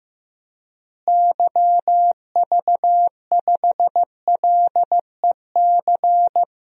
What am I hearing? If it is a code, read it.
Morse code, heard as a keyed tone, about 15 wpm, 700 Hz. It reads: YV5LEC
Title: DXCC-100 Call Signs - 6 Characters - Repeated 3x - 15wpm